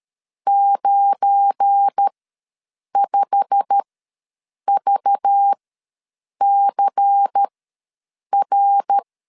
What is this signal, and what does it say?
Morse code, heard as a keyed tone, about 13 wpm, spaced as for 7 wpm, 800 Hz. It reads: 95VCR
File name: LEZIONI / 8 YJ45 / 8 YJ45 ws 12wpm 3146